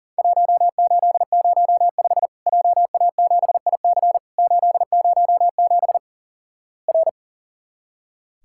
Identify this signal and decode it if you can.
Morse code, heard as a keyed tone, about 37 wpm, 700 Hz. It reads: 1805 JA7IC 807 R
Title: Worked All Europe DX Contest - QTC Exchange - 40wpm